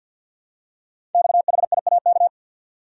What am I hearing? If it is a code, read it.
Morse code, heard as a keyed tone, about 52 wpm, 700 Hz. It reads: XHIAK
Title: Code Groups - 5 Letter:Numbers - 50wpm